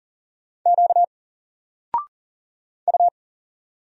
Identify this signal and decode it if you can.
Morse code, heard as a keyed tone, about 40 wpm, 700 Hz. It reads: Q U
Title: Single Letter-Number 40wpm